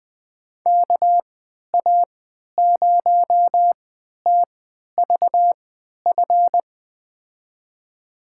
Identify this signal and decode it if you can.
Morse code, heard as a keyed tone, about 20 wpm, spaced as for 11 wpm, 700 Hz. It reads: KA0TVF
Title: USA Call Signs - 6 Characters - 3x Character Spacing - 20wpm